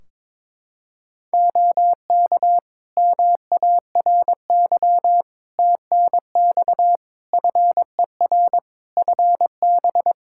Morse code, heard as a keyed tone, about 22 wpm, 700 Hz. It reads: OK MARY TNX FER FB
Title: Practice QSO - Exchange 4 - Ending - 22wpm